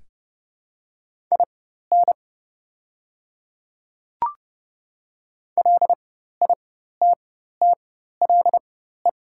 Morse code, heard as a keyed tone, about 30 wpm, spaced as for 14 wpm, 700 Hz. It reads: IN LITTLE
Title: 4x Character Spacing - Top 200 Words - 30wpm